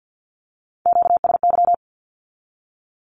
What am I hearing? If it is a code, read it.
Morse code, heard as a keyed tone, about 50 wpm, 700 Hz. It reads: QSY
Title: Q-Codes with Spoken Meaning - 50wpm